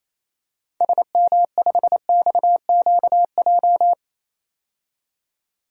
Morse code, heard as a keyed tone, about 28 wpm, 700 Hz. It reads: SM5XQJ
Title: DXCC-100 Call Signs - 6 Characters - 28wpm